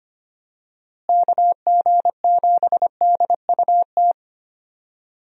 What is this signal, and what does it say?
Morse code, heard as a keyed tone, about 25 wpm, 700 Hz. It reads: KG7DUT
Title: USA Call Signs - 6 Characters - Repeated 3x - 25wpm